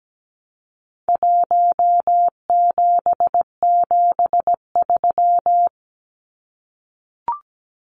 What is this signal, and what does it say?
Morse code, heard as a keyed tone, about 17 wpm, 700 Hz. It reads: 1773
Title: Numbers - Contest Style - 4-Digit 17wpm